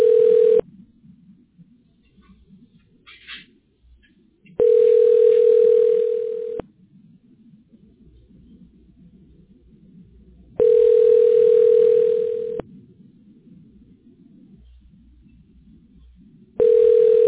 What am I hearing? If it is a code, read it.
Ringback tone (North America), 440 + 480 Hz.